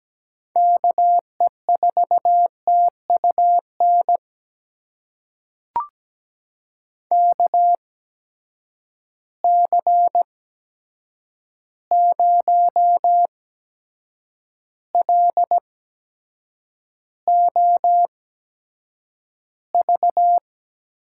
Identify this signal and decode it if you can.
Morse code, heard as a keyed tone, about 17 wpm, spaced as for 7 wpm, 700 Hz. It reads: KE4TUN KC0LOV
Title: USA Call Signs - 6 Characters - 8x Character Spacing - 17wpm